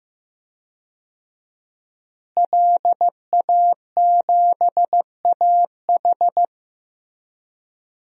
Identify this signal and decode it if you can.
Morse code, heard as a keyed tone, about 15 wpm, 700 Hz. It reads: LA7AH
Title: DXCC-100 Call Signs - 5 Characters - Repeated 2x - 15wpm